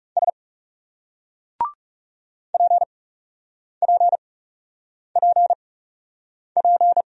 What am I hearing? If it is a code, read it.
Morse code, heard as a keyed tone, about 39 wpm, spaced as for 8 wpm, 700 Hz. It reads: S PPPP